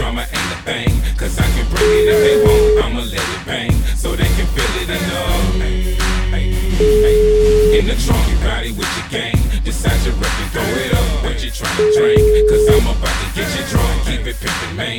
Ringback tone (Europe, 425 Hz), with noise.